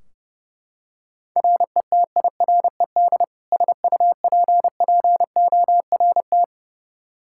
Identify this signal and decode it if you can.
Morse code, heard as a keyed tone, about 32 wpm, 700 Hz. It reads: RETIRED SUPPORT
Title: K1USN SST QSO Practice 30wpm